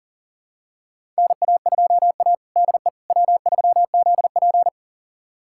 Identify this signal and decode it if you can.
Morse code, heard as a keyed tone, about 38 wpm, 700 Hz. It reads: NA2A DE W3ZP
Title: USA Callsign DE Callsign - 4 Characters - 40wpm